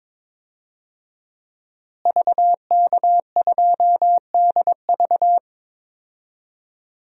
Morse code, heard as a keyed tone, about 22 wpm, 700 Hz. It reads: VK2DV
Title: DXCC-100 Call Signs - 5 Characters - Repeated 2x - 22wpm